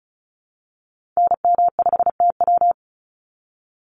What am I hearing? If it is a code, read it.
Morse code, heard as a keyed tone, about 34 wpm, 700 Hz. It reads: NM5TW
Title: NA SOTA Activator Callsigns 35wpm